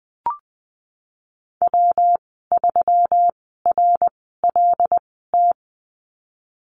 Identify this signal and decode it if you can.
Morse code, heard as a keyed tone, about 20 wpm, 700 Hz. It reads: W 3 R L T
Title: 154 Lesson 25g - Letter p - Call Signs 20wpm